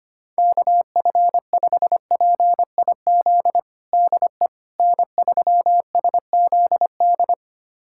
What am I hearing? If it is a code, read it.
Morse code, heard as a keyed tone, about 25 wpm, 700 Hz. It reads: KF5PIZ DE N3SZD
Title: USA Callsign DE Callsign 25wpm